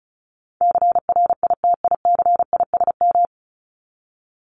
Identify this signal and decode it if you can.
Morse code, heard as a keyed tone, about 34 wpm, 700 Hz. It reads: CRITICISM
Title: Top 1001-1250 Words 35wpm